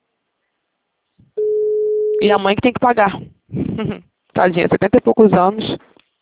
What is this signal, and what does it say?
Ringback tone (Europe, 425 Hz), with speech.